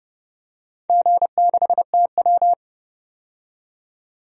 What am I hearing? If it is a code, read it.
Morse code, heard as a keyed tone, about 30 wpm, 700 Hz. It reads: G6TW
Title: DXCC-100 Call Signs - 4 Characters - 30wpm